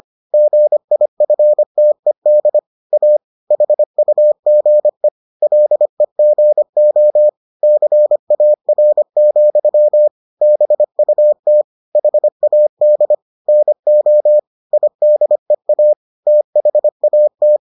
Morse code, heard as a keyed tone, about 25 wpm, 600 Hz. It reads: GIFTED A HUGE LEGO CAR, BUT HAD NO IDEA THAT